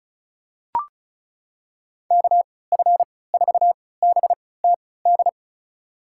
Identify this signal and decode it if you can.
Morse code, heard as a keyed tone, about 34 wpm, spaced as for 20 wpm, 700 Hz. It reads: KF4BTD
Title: USA Call Signs - 6 Characters - 3x Character Spacing - 35wpm